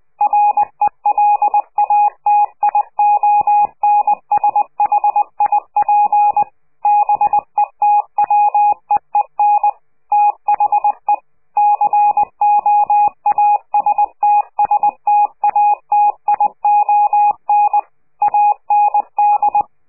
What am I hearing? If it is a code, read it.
Morse code, heard as a keyed tone, about 20 wpm, 830 Hz. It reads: RELATIONSHIP BETWEEN THE COASTSTATION AND
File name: CQ de PCH laatste bericht farewell